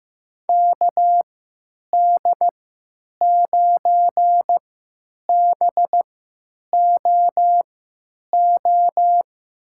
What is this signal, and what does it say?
Morse code, heard as a keyed tone, about 15 wpm, spaced as for 9 wpm, 700 Hz. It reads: KD9BOO
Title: USA Call Signs - 3x Character Spacing - 15wpm